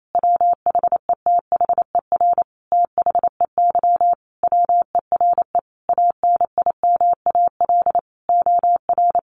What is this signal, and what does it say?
Morse code, heard as a keyed tone, about 28 wpm, 700 Hz. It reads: WHETHER THEY WERE ANIMAL OR
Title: Sentences from Top 1300 Words 28wpm